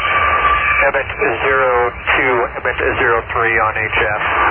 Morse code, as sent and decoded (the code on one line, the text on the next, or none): none